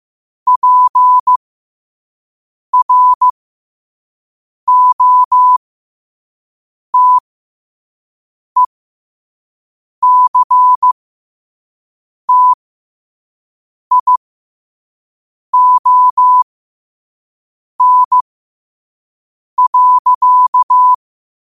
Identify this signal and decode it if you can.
Morse code, heard as a keyed tone, about 15 wpm, spaced as for 5 wpm, 1000 Hz. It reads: PROTECTION.
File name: qotdfiles / 5wpm